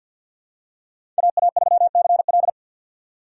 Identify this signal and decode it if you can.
Morse code, heard as a keyed tone, about 52 wpm, 690 Hz. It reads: AA3CL
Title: USA Call Signs - 6x Character Spacing - 50wpm